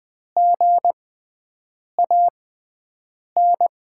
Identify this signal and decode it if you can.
Morse code, heard as a keyed tone, about 21 wpm, spaced as for 7 wpm, 700 Hz. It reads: GAN
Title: USA Call Signs - 6x Character Spacing - 20wpm